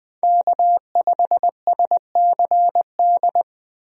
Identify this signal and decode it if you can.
Morse code, heard as a keyed tone, about 20 wpm, 700 Hz. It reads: K5SCD
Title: USA Call Signs - 5 Characters - Repeated 2x - 20wpm